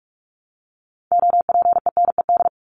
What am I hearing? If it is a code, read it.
Morse code, heard as a keyed tone, about 44 wpm, 700 Hz. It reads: OPENED
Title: ED Suffix - Top 100 Words with Base Words 45wpm